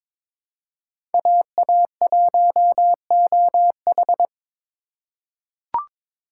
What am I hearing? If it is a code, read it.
Morse code, heard as a keyed tone, about 22 wpm, 700 Hz. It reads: AA1OH
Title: USA Call Signs - 5 Characters - 2x Character Spacing - 22wpm